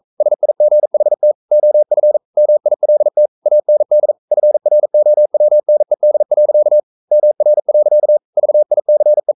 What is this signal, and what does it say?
Morse code, heard as a keyed tone, about 42 wpm, 600 Hz. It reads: SIGHT OF MILT AND FROWNED. MR. VICE
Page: Romanen Never af Ken Follet på engelsk.